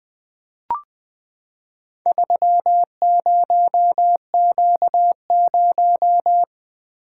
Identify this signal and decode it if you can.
Morse code, heard as a keyed tone, about 20 wpm, 700 Hz. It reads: 30Q0
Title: Four Letter-Number Combinations - 20wpm